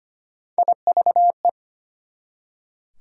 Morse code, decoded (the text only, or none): IVE